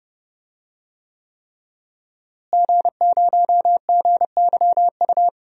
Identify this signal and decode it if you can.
Morse code, heard as a keyed tone, about 30 wpm, 700 Hz. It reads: G0GYU
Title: DXCC-100 Call Signs - 5 Characters - Repeated 3x - 30wpm